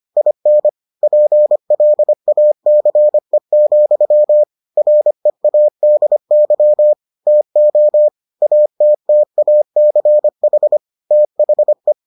Morse code, heard as a keyed tone, about 25 wpm, 600 Hz. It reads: IN PLACE, READY TO ATTACH THE